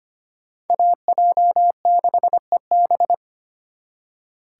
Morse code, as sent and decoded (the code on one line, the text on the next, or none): .- .--- -.... -...
AJ6EB